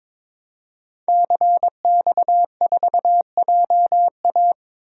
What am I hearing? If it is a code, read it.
Morse code, heard as a keyed tone, about 22 wpm, 700 Hz. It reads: CX4JA
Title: DXCC-100 Call Signs - 5 Characters - Repeated 2x - 22wpm